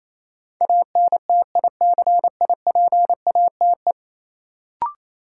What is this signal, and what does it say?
Morse code, heard as a keyed tone, about 28 wpm, 700 Hz. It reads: ANTICIPATE
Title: Sentences from Top 1500 Words - New Word Drill - 28wpm